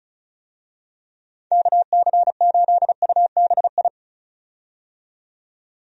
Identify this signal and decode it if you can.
Morse code, heard as a keyed tone, about 36 wpm, 700 Hz. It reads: KC8UBI
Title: USA Call Signs - 6 Characters - 35wpm